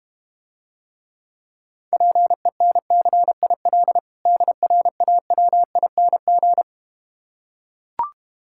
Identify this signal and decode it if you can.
Morse code, heard as a keyed tone, about 32 wpm, 690 Hz. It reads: PENCIL DRAWING